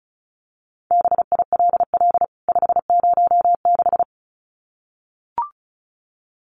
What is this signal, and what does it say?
Morse code, heard as a keyed tone, about 34 wpm, 700 Hz. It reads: BILL 506
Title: CWT Exchange 35wpm